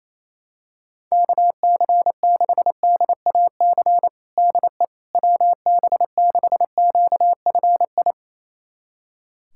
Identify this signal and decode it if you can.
Morse code, heard as a keyed tone, about 28 wpm, 700 Hz. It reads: KC6DAC DE WB6QFI
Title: USA Callsign DE Callsign - 6 Characters - 28wpm